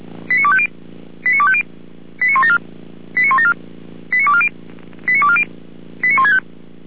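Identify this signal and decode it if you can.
Five-tone selective call (ZVEI), heard as quick runs of five tones, tones of about 70 ms. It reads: 89250 89250 89174 89174 89240 89240 89165
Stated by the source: ZVEI five-tone selective call 89337 received on SDR, from fire brigade Feuerwehr Orlamünde.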